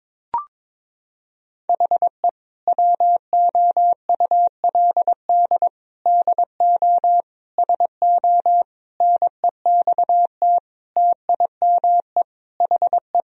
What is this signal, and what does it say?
Morse code, heard as a keyed tone, about 22 wpm, 700 Hz. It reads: HE WOULD DO SO NEXT TIME HE
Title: Sentences from Top 1000 Words - Review Part 1 - 22wpm